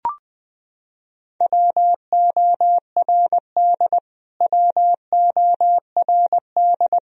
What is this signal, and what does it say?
Morse code, heard as a keyed tone, about 20 wpm, 700 Hz. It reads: WORD WORD
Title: Top 100 Words Repeated 20wpm